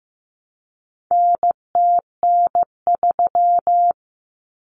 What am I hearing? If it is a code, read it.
Morse code, heard as a keyed tone, about 15 wpm, 700 Hz. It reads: NTN3
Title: Common Cut Numbers - 4-Digit 15wpm